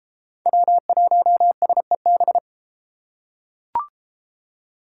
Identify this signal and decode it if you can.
Morse code, heard as a keyed tone, about 30 wpm, 700 Hz. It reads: W1SEB